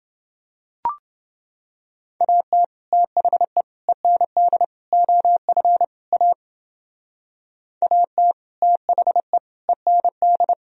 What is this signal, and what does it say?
Morse code, heard as a keyed tone, about 29 wpm, 700 Hz. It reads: AT THE END OF A AT THE END